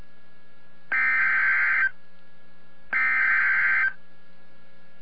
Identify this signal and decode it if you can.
Emergency Alert System (SAME) tones (header bursts), heard as harsh buzzing bursts. It is heard from 0.9 to 3.8 s.